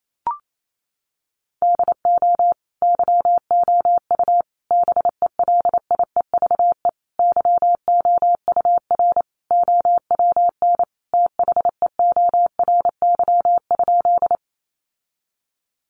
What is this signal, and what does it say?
Morse code, heard as a keyed tone, about 28 wpm, 700 Hz. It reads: DO YOU BELIEVE YOUR OWN THEORY?
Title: Sentences from Top 1000 Words 28wpm